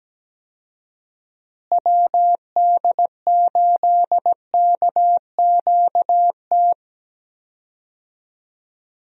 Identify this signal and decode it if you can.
Morse code, heard as a keyed tone, about 17 wpm, 700 Hz. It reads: WD8KQT
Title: USA Call Signs - 6 Characters - Repeated 3x - 17wpm